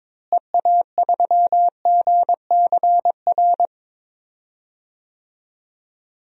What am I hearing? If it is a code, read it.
Morse code, heard as a keyed tone, about 22 wpm, 700 Hz. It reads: EA3GCR
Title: DXCC-100 Call Signs - 6 Characters - Repeated 3x - 22wpm